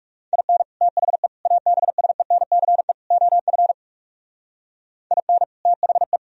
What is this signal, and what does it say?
Morse code, heard as a keyed tone, about 37 wpm, 700 Hz. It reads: IN THE ABSENCE OF IN THE
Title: Speed-Racing - Sets of 4 Words 30wpm